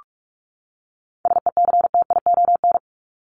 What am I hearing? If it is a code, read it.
Morse code, heard as a keyed tone, about 44 wpm, 700 Hz. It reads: SECTION
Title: Top 501-750 Words 45wpm